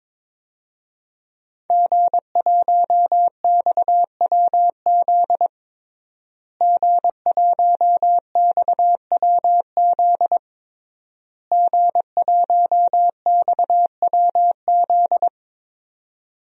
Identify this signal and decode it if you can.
Morse code, heard as a keyed tone, about 22 wpm, 700 Hz. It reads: G1XWZ G1XWZ G1XWZ
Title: DXCC-100 Call Signs - 5 Characters - Repeated 3x - 22wpm